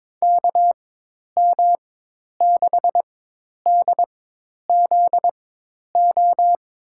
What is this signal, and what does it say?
Morse code, heard as a keyed tone, about 22 wpm, spaced as for 10 wpm, 700 Hz. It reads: KM6DZO
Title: USA Call Signs - 6 Characters - 4x Character Spacing - 22wpm